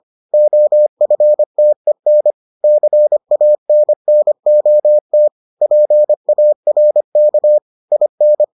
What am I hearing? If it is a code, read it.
Morse code, heard as a keyed tone, about 25 wpm, 600 Hz. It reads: OFTEN CANNOT PARK IN